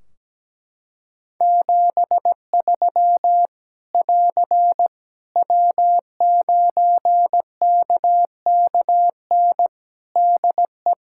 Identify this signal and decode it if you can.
Morse code, heard as a keyed tone, about 17 wpm, 700 Hz. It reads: 73 <AR> W9KKN DE
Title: Practice QSO - Exchange 4 - Ending - 17wpm